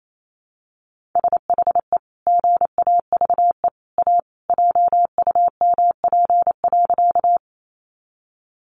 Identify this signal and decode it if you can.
Morse code, heard as a keyed tone, about 28 wpm, 700 Hz. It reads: SHE GAVE A JUMP.
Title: Sentences from Top 1300 Words 28wpm